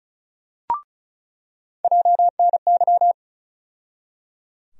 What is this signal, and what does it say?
Morse code, heard as a keyed tone, about 34 wpm, 700 Hz. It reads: JNY
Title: USA Call Sign Postfixes - 3 Letters 35wpm